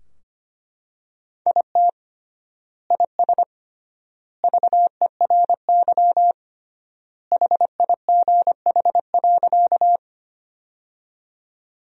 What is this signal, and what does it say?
Morse code, heard as a keyed tone, about 25 wpm, 700 Hz. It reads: IT IS VERY HIGH.
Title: 3x Word-Spacing - Sentences from Top 200 Words - 25wpm